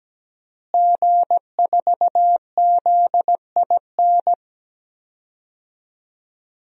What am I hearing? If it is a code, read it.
Morse code, heard as a keyed tone, about 17 wpm, 700 Hz. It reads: G4ZIN
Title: DXCC-100 Call Signs - 5 Characters - Repeated 2x - 17wpm